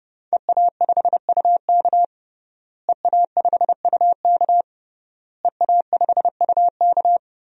Morse code, heard as a keyed tone, about 30 wpm, 700 Hz. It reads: EA5UK EA5UK EA5UK
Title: DXCC-100 Call Signs - 5 Characters - Repeated 3x - 30wpm